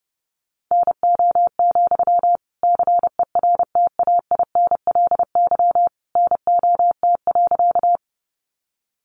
Morse code, heard as a keyed tone, about 30 wpm, 700 Hz. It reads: NO, CERTAINLY NOT.
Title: Sentences from Top 900 Words 30wpm